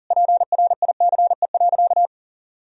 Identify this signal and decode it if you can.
Morse code, heard as a keyed tone, about 40 wpm, 700 Hz. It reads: PRICE.
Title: Sentences from Top 1000 Words - Review Part 4 - 40wpm